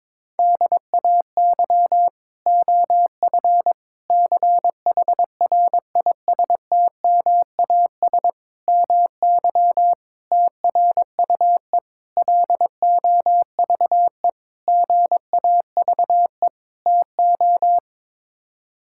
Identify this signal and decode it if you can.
Morse code, heard as a keyed tone, about 21 wpm, 700 Hz. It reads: DAY OF CHRISTMAS MY TRUE LOVE GAVE TO
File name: practice / 22wpm / Sets of 8 Words - 22wpm